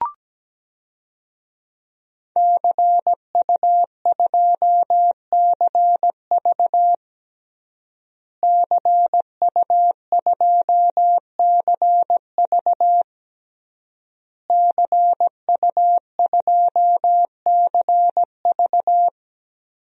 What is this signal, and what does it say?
Morse code, heard as a keyed tone, about 17 wpm, 700 Hz. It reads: CU2CV CU2CV CU2CV
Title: DXCC-100 Call Signs - 5 Characters - Repeated 3x - 17wpm